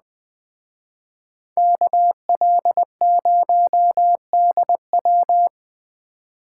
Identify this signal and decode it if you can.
Morse code, heard as a keyed tone, about 20 wpm, 700 Hz. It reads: KL0DW